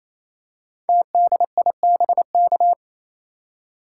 Morse code, heard as a keyed tone, about 27 wpm, 700 Hz. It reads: TDIBK
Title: Code Groups - 5 Letters - 28wpm